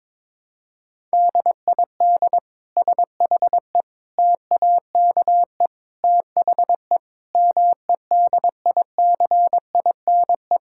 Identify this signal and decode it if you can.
Morse code, heard as a keyed tone, about 22 wpm, 700 Hz. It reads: DID SHE TAKE THE MEDICINE
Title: Sentences from Top 1000 Words - Review Part 6 - 22wpm